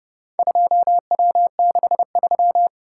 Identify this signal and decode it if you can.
Morse code, heard as a keyed tone, about 29 wpm, 700 Hz. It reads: 2W63